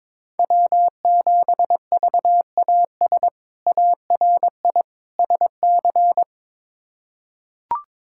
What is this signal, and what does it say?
Morse code, heard as a keyed tone, about 22 wpm, 700 Hz. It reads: W7VAS ARI SC
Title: K1USN SST QSO Practice - 22wpm